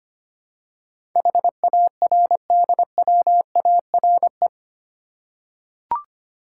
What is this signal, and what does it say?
Morse code, heard as a keyed tone, about 25 wpm, 700 Hz. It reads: HARDWARE
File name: 8 Letter Words - First 3 Letters Given - 25wpm